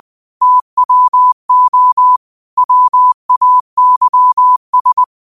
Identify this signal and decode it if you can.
Morse code, heard as a keyed tone, about 20 wpm, 1000 Hz. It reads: TWO WAYS
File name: qotdfiles / 20wpm